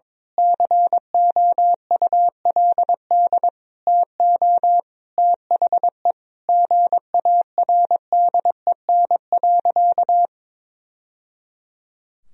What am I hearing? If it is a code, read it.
Morse code, heard as a keyed tone, about 22 wpm, 700 Hz. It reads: COULD TO THE GARDEN.
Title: Sentences from Top 1000 Words - Review Part 4 - 22wpm